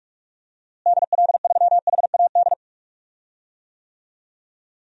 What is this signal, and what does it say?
Morse code, heard as a keyed tone, about 46 wpm, 700 Hz. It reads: DL3HAD
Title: DXCC-100 Call Signs - 6 Characters - Repeated 2x - 45wpm